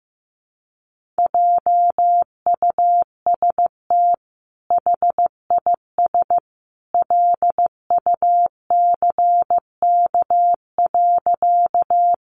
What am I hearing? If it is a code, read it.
Morse code, heard as a keyed tone, about 15 wpm, 700 Hz. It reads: JUST HIS LUCK.
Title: Sentences from Top 1500 Words 15wpm